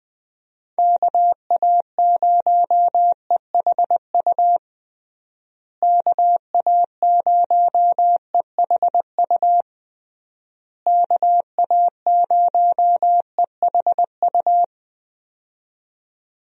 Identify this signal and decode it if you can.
Morse code, heard as a keyed tone, about 21 wpm, 700 Hz. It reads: KA0EHU KA0EHU KA0EHU
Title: USA Call Signs - 6 Characters - Repeated 3x - 20wpm